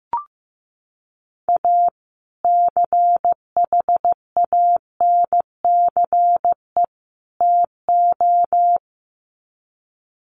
Morse code, heard as a keyed tone, about 15 wpm, 700 Hz. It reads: A CHANCE TO
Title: Sets of 3 Words 15wpm